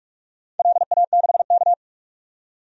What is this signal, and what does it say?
Morse code, heard as a keyed tone, about 47 wpm, 700 Hz. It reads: LA6K
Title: DXCC-100 Call Signs - 4 Characters - 45wpm